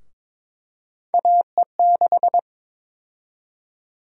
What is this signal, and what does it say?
Morse code, heard as a keyed tone, about 22 wpm, 700 Hz. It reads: AE6
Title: USA Call Sign Prefixes - Repeated 2x - 22wpm